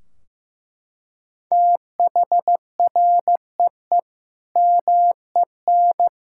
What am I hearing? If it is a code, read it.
Morse code, heard as a keyed tone, about 14 wpm, 700 Hz. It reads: THREE MEN
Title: Sentences from Top 1600 Words 15wpm